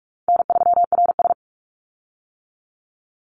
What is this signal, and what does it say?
Morse code, heard as a keyed tone, about 44 wpm, 700 Hz. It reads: N3RS
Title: Easy Call Signs - 45wpm